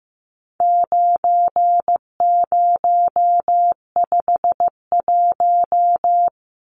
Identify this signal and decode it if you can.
Morse code, heard as a keyed tone, about 15 wpm, 700 Hz. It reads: 9051
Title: Numbers - 4-Digit 15wpm